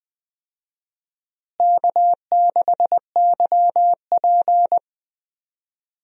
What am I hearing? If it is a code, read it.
Morse code, heard as a keyed tone, about 20 wpm, 700 Hz. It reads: K6YP
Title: USA Call Signs - 4 Characters - Repeated 2x - 20wpm